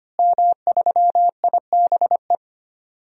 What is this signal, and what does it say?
Morse code, heard as a keyed tone, about 25 wpm, 700 Hz. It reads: M3IBE
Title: DXCC-100 Call Signs - 5 Characters - Repeated 3x - 25wpm